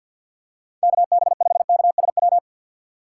Morse code, heard as a keyed tone, about 53 wpm, 700 Hz. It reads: KB5KSW
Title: USA Call Signs - 6 Characters - 6x Character Spacing - 50wpm